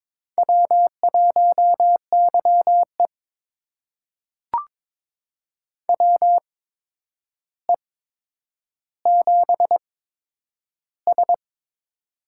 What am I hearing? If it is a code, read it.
Morse code, heard as a keyed tone, about 22 wpm, spaced as for 9 wpm, 700 Hz. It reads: W1YE WE7S
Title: USA Call Signs - 4 Characters - 8x Character Spacing - 22wpm